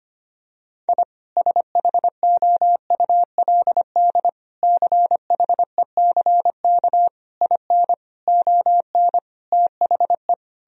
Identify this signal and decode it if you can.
Morse code, heard as a keyed tone, about 25 wpm, 700 Hz. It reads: I SHOULD CHECK IN ON THE
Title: Sentences from Top 1000 Words - Review Part 3 - 25wpm